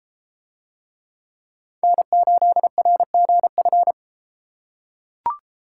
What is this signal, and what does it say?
Morse code, heard as a keyed tone, about 31 wpm, 700 Hz. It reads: N8RGF